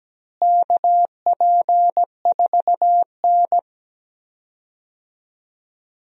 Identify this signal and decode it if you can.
Morse code, heard as a keyed tone, about 17 wpm, 700 Hz. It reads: KP4N
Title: DXCC-100 Call Signs - 4 Characters - Repeated 3x - 17wpm